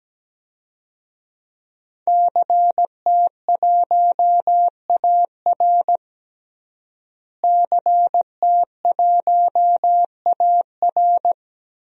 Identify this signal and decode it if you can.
Morse code, heard as a keyed tone, about 17 wpm, 700 Hz. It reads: CT1AR CT1AR
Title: DXCC-100 Call Signs - 5 Characters - Repeated 3x - 17wpm